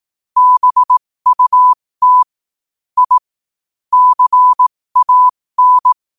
Morse code, heard as a keyed tone, about 18 wpm, spaced as for 15 wpm, 1000 Hz. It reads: BUT I CAN
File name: qotdfiles / 15wpm